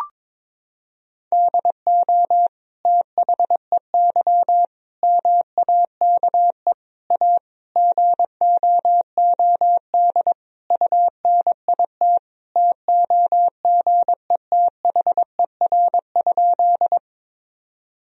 Morse code, heard as a keyed tone, about 22 wpm, 700 Hz. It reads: DO THEY MAKE A GOOD UNIT TOGETHER?
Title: Sentences from Top 1000 Words - Review Part 2 - 22wpm